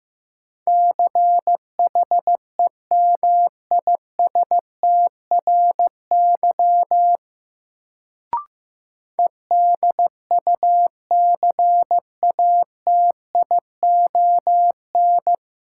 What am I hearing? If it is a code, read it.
Morse code, heard as a keyed tone, about 15 wpm, 700 Hz. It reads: CHEMISTRY EDUCATION
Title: Sentences from Top 1000 Words - Review Part 6 - New Word Drill - 15wpm